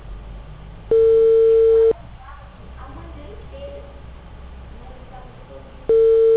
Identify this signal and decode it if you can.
Ringback tone (China, 450 Hz), with speech.
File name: out- -2011- - - .13477